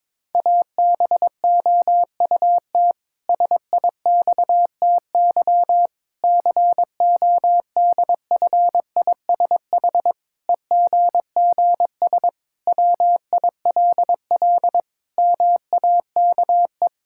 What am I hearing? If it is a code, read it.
Morse code, heard as a keyed tone, about 22 wpm, 700 Hz. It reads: ABOUT SIXTY CODFISH EGGS WILL MAKE
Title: Pangram Challenge v2 - 22wpm